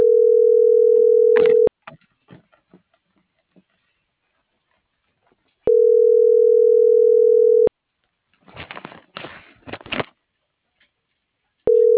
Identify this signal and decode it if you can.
Ringback tone (North America), 440 + 480 Hz.
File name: CENTRAL SAGOT / out diego.emir